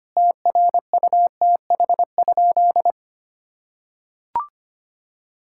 Morse code, heard as a keyed tone, about 25 wpm, 700 Hz. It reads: TRUTH?
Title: Sentences from Top 1600 Words 25wpm